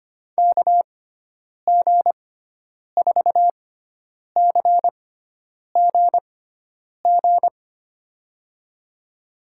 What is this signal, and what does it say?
Morse code, heard as a keyed tone, about 25 wpm, spaced as for 9 wpm, 700 Hz. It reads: KG4CGG
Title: USA Call Signs - 6 Characters - 6x Character Spacing - 25wpm